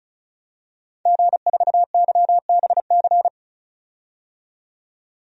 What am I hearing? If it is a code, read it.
Morse code, heard as a keyed tone, about 34 wpm, 700 Hz. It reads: G4YBC
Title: DXCC-100 Call Signs - 5 Characters - 35wpm